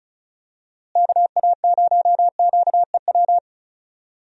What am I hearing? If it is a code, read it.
Morse code, heard as a keyed tone, about 34 wpm, 700 Hz. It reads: KA0QEW
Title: USA Call Signs - 6 Characters - Repeated 2x - 35wpm